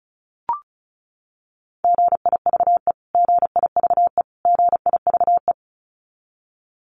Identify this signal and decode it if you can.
Morse code, heard as a keyed tone, about 34 wpm, 700 Hz. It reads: GIVE GIVE GIVE
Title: Top 500 Words Repeated 35wpm